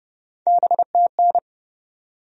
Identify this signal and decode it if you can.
Morse code, heard as a keyed tone, about 30 wpm, 700 Hz. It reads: BTN
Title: USA Call Sign Postfixes - 3 Letters 30wpm